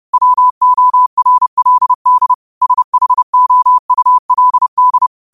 Morse code, heard as a keyed tone, about 30 wpm, 1000 Hz. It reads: WORLD SHOULD
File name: qotdfiles / 30wpm